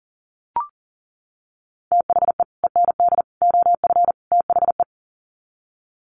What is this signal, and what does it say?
Morse code, heard as a keyed tone, about 40 wpm, 700 Hz. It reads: THE END OF THE
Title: Sets of 4 Words 40wpm